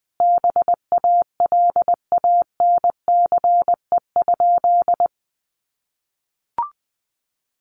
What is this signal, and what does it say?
Morse code, heard as a keyed tone, about 20 wpm, 700 Hz. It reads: BALANCE?
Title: Sentences from Top 900 Words 20wpm